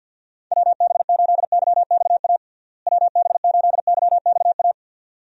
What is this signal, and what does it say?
Morse code, heard as a keyed tone, about 51 wpm, 700 Hz. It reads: WB8YXA WB8YXA
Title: USA Call Signs - 6 Characters - Repeated 2x - 50wpm